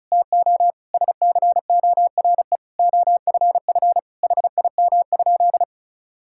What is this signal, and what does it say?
Morse code, heard as a keyed tone, about 36 wpm, 700 Hz. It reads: TO SCORE OFF HIM?
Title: Sentences from Top 1600 Words 35wpm